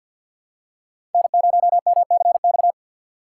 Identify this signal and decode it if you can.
Morse code, heard as a keyed tone, about 52 wpm, 700 Hz. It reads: N0MCX
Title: USA Call Signs - 5 Characters - 6x Character Spacing - 50wpm